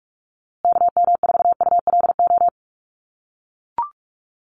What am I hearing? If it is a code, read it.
Morse code, heard as a keyed tone, about 44 wpm, 700 Hz. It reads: KM4ULO
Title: USA Call Signs Repeated 3x 45wpm